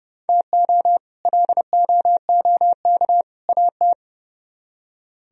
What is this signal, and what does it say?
Morse code, heard as a keyed tone, about 37 wpm, 700 Hz. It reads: TO LOOK AT
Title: Speed-Racing - Sets of 3 Words 30wpm